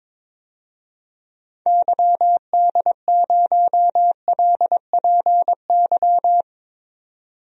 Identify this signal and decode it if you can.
Morse code, heard as a keyed tone, about 22 wpm, 700 Hz. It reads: YD0LPY
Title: DXCC-100 Call Signs - 6 Characters - 22wpm